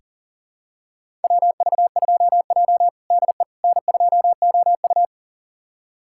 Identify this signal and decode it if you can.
Morse code, heard as a keyed tone, about 40 wpm, 690 Hz. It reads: WV2J DE N2OU
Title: USA Callsign DE Callsign - 4 Characters - 40wpm